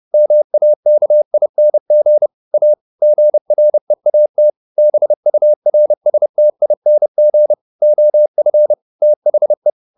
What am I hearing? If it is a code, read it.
Morse code, heard as a keyed tone, about 30 wpm, 600 Hz. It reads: MAKING A GREAT BURSTING OF THE